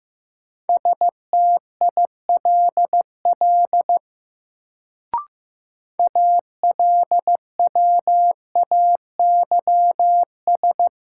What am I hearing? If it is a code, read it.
Morse code, heard as a keyed tone, about 15 wpm, 700 Hz. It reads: STILL ALWAYS
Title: Sentences from Top 300 Words - New Word Drill - 15wpm